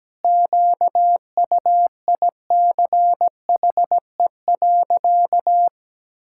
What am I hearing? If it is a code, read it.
Morse code, heard as a keyed tone, about 17 wpm, 700 Hz. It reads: QUICHE.
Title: Pangram Challenge v2 - 17wpm